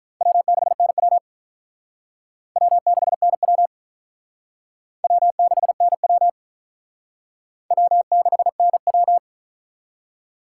Morse code, heard as a keyed tone, about 45 wpm, 700 Hz. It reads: W6NW W6NW W6NW W6NW